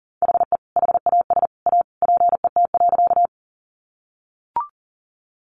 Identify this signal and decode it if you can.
Morse code, heard as a keyed tone, about 40 wpm, 700 Hz. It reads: HE HAS A PET.
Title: Easy Phrases 40wpm